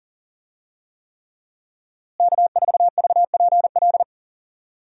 Morse code, heard as a keyed tone, about 37 wpm, 700 Hz. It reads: K4VPL
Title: USA Call Signs - 5 Characters - Repeated 3x - 40wpm